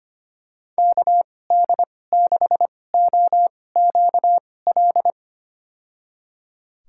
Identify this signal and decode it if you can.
Morse code, heard as a keyed tone, about 25 wpm, spaced as for 18 wpm, 700 Hz. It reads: KD6OQL
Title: USA Call Signs - 6 Characters - 2x Character Spacing - 25wpm